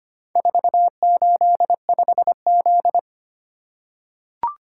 Morse code, heard as a keyed tone, about 25 wpm, 700 Hz. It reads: 485Z
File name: practice / 25wpm / Four Letter-Number Combinations - 25wpm